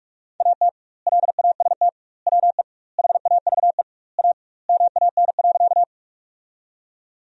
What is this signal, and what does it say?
Morse code, heard as a keyed tone, about 46 wpm, 700 Hz. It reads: AT LAST WE HAVE A MAN.
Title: 2x Word-Spacing - Sentences from Top 300 Words - 45wpm